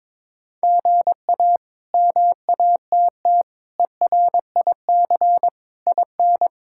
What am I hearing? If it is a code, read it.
Morse code, heard as a keyed tone, about 22 wpm, 700 Hz. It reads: GA MATT ERIC IN
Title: K1USN SST QSO Practice - 22wpm